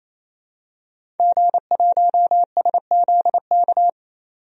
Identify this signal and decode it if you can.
Morse code, heard as a keyed tone, about 28 wpm, 700 Hz. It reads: G1SZK
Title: DXCC-100 Call Signs - 5 Characters - 28wpm